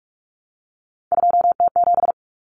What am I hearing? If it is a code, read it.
Morse code, heard as a keyed tone, about 44 wpm, 700 Hz. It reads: 2T7
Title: Common Cut Numbers - 3-Digit 45wpm